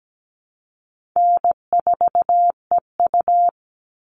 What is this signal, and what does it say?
Morse code, heard as a keyed tone, about 17 wpm, 700 Hz. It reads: N4EU
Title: All Cut Numbers - 4-Digit 17wpm